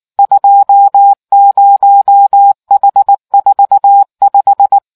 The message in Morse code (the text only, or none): 20H45